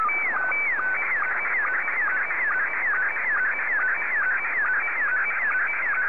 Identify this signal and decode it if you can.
SSTV (slow-scan television), Scottie 1.